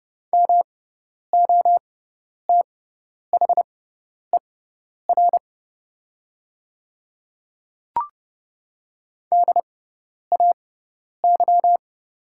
Morse code, heard as a keyed tone, about 30 wpm, spaced as for 10 wpm, 700 Hz. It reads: MOTHER DAY
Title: 6x Character Spacing - Top 300 Words - 30wpm